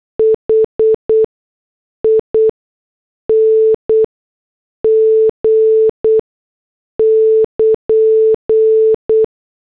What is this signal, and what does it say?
Morse code, heard as a keyed tone, about 8 wpm, spaced as for 6 wpm, 440 Hz. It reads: HING <KN>